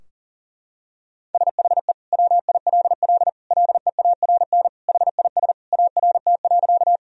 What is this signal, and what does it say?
Morse code, heard as a keyed tone, about 40 wpm, 700 Hz. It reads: SHE WILL LEARN HIS ART.
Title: Sentences from Top 1000 Words - Review Part 3 - 40wpm